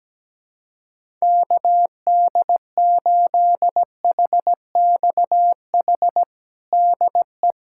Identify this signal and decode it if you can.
Morse code, heard as a keyed tone, about 17 wpm, 700 Hz. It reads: KD8HXH DE
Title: USA Callsign DE Callsign 17wpm